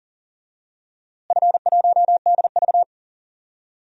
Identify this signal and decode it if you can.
Morse code, heard as a keyed tone, about 40 wpm, 700 Hz. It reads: F1DV
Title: DXCC-100 Call Signs - 4 Characters - Repeated 2x - 40wpm